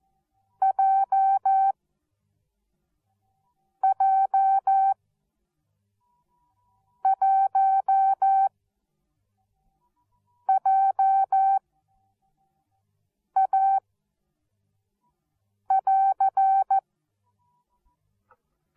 Morse code, heard as a keyed tone, about 14 wpm, spaced as for 4 wpm, 770 Hz. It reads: JJ1JA<AR>
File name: Onlinelehrgang / morsen / DJ4UF Morsen13